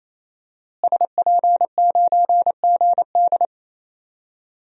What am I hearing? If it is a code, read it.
Morse code, heard as a keyed tone, about 28 wpm, 700 Hz. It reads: SP9GD
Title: DXCC-100 Call Signs - 5 Characters - Repeated 3x - 28wpm